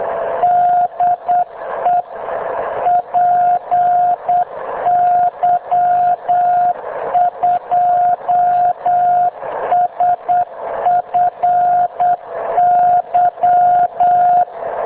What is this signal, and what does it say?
Morse code, heard as a keyed tone, about 8 wpm, 710 Hz. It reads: DE PY2SFY
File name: Enterprises / bpy2sfy